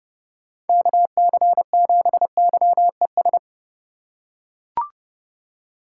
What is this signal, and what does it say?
Morse code, heard as a keyed tone, about 30 wpm, 700 Hz. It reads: KC7YES
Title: USA Call Signs - 6 Characters - 2x Character Spacing - 30wpm